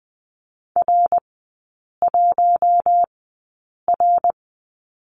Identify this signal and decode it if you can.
Morse code, heard as a keyed tone, about 20 wpm, spaced as for 8 wpm, 700 Hz. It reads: R1R
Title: 053 Lesson 10e - Letter r - Three Characters 20wpm